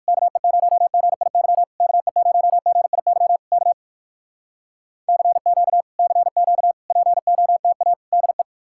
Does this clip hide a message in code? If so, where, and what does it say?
Yes: Morse code from 0.1 to 8.4 s, heard as a keyed tone, about 66 wpm, spaced as for 57 wpm, 700 Hz. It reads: KE0GIY KE0GIY K CQ CQ POTA DE